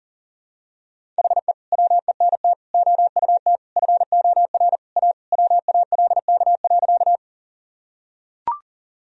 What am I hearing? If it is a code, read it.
Morse code, heard as a keyed tone, about 40 wpm, 700 Hz. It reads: HE WENT OUT FOR A WALK.
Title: Sentences from Top 1000 Words - Review Part 1 - 40wpm